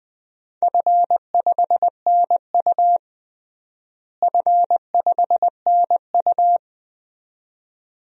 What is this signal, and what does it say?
Morse code, heard as a keyed tone, about 20 wpm, 700 Hz. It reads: F5NU F5NU
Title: DXCC-100 Call Signs - 4 Characters - Repeated 2x - 20wpm